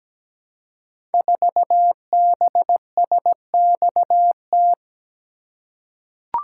Morse code, heard as a keyed tone, about 17 wpm, 700 Hz. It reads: 4BSXT
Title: Code Groups - 5 Letter:Numbers - 17wpm